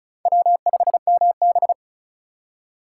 Morse code, heard as a keyed tone, about 43 wpm, 700 Hz. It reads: W5MB
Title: Speed Racing - USA Call Signs 35wpm